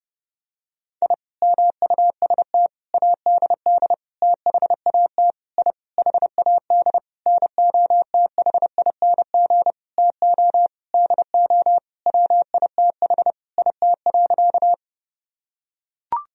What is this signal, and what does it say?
Morse code, heard as a keyed tone, about 30 wpm, 700 Hz. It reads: I MUST ADD THAT I HAD NOTHING TO DO WITH IT.
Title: Sentences from Top 1000 Words - Review Part 1 - 30wpm